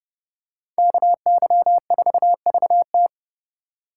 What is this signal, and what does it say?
Morse code, heard as a keyed tone, about 29 wpm, 700 Hz. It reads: KY4VT